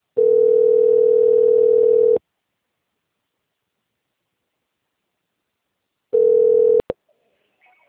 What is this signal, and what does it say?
Ringback tone (North America), 440 + 480 Hz.
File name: out- - - - - .81507